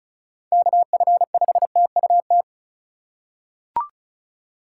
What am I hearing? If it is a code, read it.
Morse code, heard as a keyed tone, about 36 wpm, 700 Hz. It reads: KF5TUT
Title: USA Call Signs - 6 Characters - 4x Character Spacing - 35wpm